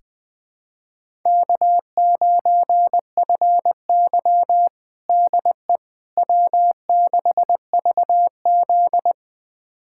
Morse code, heard as a keyed tone, about 20 wpm, 700 Hz. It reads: K9FY DE W6VZ
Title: USA Callsign DE Callsign - 4 Characters - 20wpm